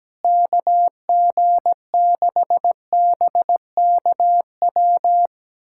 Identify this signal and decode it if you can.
Morse code, heard as a keyed tone, about 17 wpm, 700 Hz. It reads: KG6BKW
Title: USA Callsign DE Callsign 17wpm